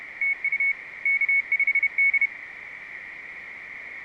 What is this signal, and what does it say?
Morse code, heard as a keyed tone, about 32 wpm, 2200 Hz. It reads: TU K5D